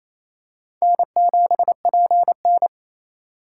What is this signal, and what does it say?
Morse code, heard as a keyed tone, about 28 wpm, 700 Hz. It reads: N7PN
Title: USA Call Signs - 4 Characters - 3x Character Spacing - 28wpm